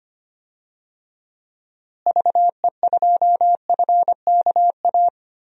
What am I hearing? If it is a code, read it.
Morse code, heard as a keyed tone, about 25 wpm, 700 Hz. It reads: VE2FKA
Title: DXCC-100 Call Signs - 6 Characters - Repeated 3x - 25wpm